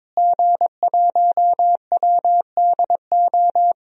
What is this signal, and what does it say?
Morse code, heard as a keyed tone, about 22 wpm, 700 Hz. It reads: G1WDO
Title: DXCC-100 Call Signs - 5 Characters - Repeated 2x - 22wpm